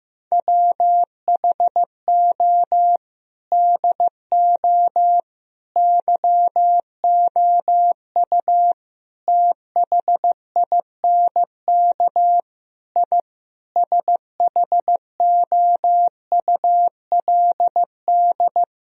Morse code, heard as a keyed tone, about 15 wpm, 700 Hz. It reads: WHO DO YOU THINK I SHOULD
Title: Sentences from Top 1000 Words - Review Part 4 - 15wpm